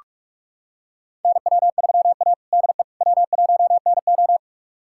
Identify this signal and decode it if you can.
Morse code, heard as a keyed tone, about 46 wpm, 700 Hz. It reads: NW3A DE W1NO
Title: USA Callsign DE Callsign - 4 Characters - 45wpm